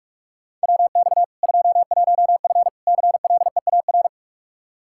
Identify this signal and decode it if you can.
Morse code, heard as a keyed tone, about 46 wpm, 700 Hz. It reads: WX 21F CLEAR
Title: K1USN SST QSO Practice 45wpm